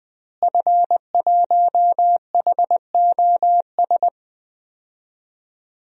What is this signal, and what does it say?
Morse code, heard as a keyed tone, about 21 wpm, 700 Hz. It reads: F1HOS
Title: DXCC-100 Call Signs - 5 Characters - Repeated 2x - 20wpm